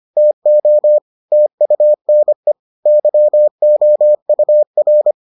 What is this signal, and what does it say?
Morse code, heard as a keyed tone, about 25 wpm, 600 Hz. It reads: TO TUNE YOUR